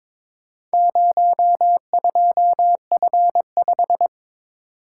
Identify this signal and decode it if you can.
Morse code, heard as a keyed tone, about 22 wpm, 700 Hz. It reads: 02F5
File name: practice / 22wpm / Four Letter-Number Combinations - 22wpm